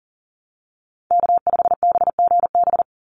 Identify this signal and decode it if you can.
Morse code, heard as a keyed tone, about 41 wpm, 700 Hz. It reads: K5BGB
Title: Call Signs - 40wpm